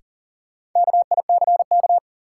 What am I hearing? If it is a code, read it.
Morse code, heard as a keyed tone, about 40 wpm, 700 Hz. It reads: KICK
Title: Sentences from Top 1300 Words - New Word Drill - 40wpm